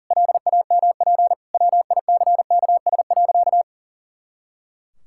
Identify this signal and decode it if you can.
Morse code, heard as a keyed tone, about 37 wpm, 700 Hz. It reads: LAMP WICKS.
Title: Pangram Challenge v2 - 40wpm